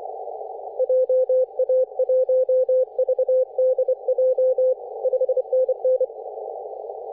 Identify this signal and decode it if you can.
Morse code, heard as a keyed tone, about 23 wpm, 510 Hz. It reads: JA1VDJ 5NN
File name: 3D2YA 24.913MHz CW